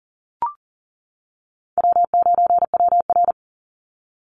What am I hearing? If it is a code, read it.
Morse code, heard as a keyed tone, about 39 wpm, 700 Hz. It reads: W9WR
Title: Easy Call Signs - 40wpm